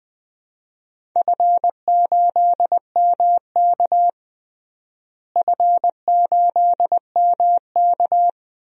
Morse code, heard as a keyed tone, about 20 wpm, 700 Hz. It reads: F8MK F8MK
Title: DXCC-100 Call Signs - 4 Characters - Repeated 2x - 20wpm